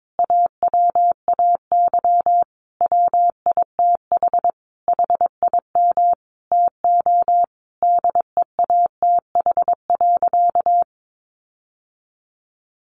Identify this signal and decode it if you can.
Morse code, heard as a keyed tone, about 22 wpm, 700 Hz. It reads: AWAY WITH HIM TO DEATH.
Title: Sentences from Top 1100 Words 22wpm